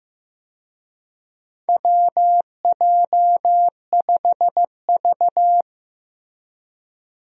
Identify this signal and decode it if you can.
Morse code, heard as a keyed tone, about 15 wpm, 700 Hz. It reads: WJ5V
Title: USA Call Signs - 4 Characters - Repeated 2x - 15wpm